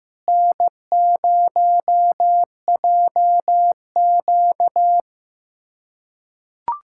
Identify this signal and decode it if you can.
Morse code, heard as a keyed tone, about 15 wpm, 700 Hz. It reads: N0JQ
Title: USA Callsign DE Callsign - 4 Characters - 15wpm